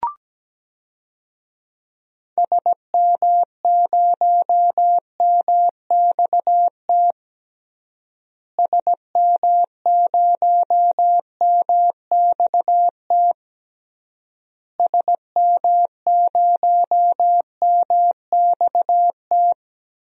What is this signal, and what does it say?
Morse code, heard as a keyed tone, about 17 wpm, 700 Hz. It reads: SM0MXT SM0MXT SM0MXT
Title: DXCC-100 Call Signs - 6 Characters - Repeated 3x - 17wpm